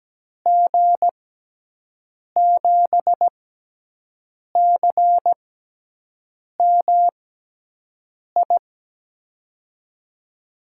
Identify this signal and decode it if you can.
Morse code, heard as a keyed tone, about 17 wpm, spaced as for 6 wpm, 700 Hz. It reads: G7CMI
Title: USA Call Signs - 6 Characters - 6x Character Spacing - 17wpm